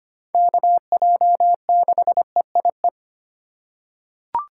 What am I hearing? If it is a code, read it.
Morse code, heard as a keyed tone, about 25 wpm, 700 Hz. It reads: KJ6EIE
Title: USA Call Signs - 6 Characters - 3x Character Spacing - 25wpm